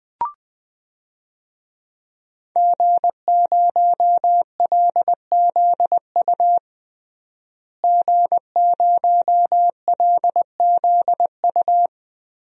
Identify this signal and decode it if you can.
Morse code, heard as a keyed tone, about 20 wpm, 700 Hz. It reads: G0LZU G0LZU
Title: DXCC-100 Call Signs - 5 Characters - Repeated 3x - 20wpm